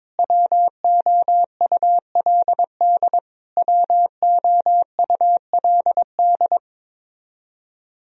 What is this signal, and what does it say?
Morse code, heard as a keyed tone, about 22 wpm, 700 Hz. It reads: WOULD WOULD
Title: Top 100 Words Repeated 22wpm